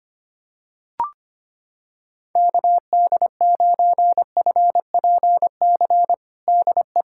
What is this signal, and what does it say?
Morse code, heard as a keyed tone, about 25 wpm, 700 Hz. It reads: KD9FPC DE
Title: USA Callsign DE Callsign - 6 Characters - 25wpm